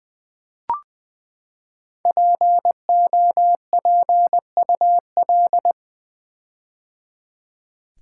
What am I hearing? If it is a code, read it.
Morse code, heard as a keyed tone, about 20 wpm, 700 Hz. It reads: POPUL
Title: Build A Word - 5 Letters At A Time - 10-Letter Words - 20wpm